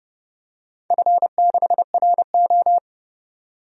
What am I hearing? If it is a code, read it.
Morse code, heard as a keyed tone, about 30 wpm, 700 Hz. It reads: F6RO
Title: DXCC-100 Call Signs - 4 Characters - 30wpm